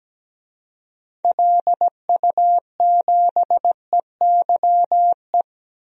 Morse code, heard as a keyed tone, about 17 wpm, 700 Hz. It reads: LU7EYE
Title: DXCC-100 Call Signs - 6 Characters - 17wpm